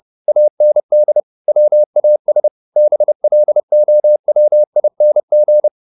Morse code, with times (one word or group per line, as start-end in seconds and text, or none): 0.3-1.2 AND
1.5-2.5 WAS
2.8-5.7 BLOWING